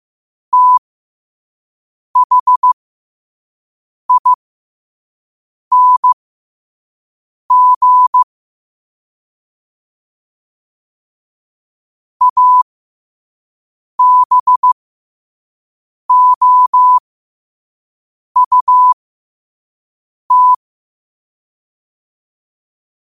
Morse code, heard as a keyed tone, about 15 wpm, spaced as for 5 wpm, 1000 Hz. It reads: THING ABOUT
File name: qotdfiles / 5wpm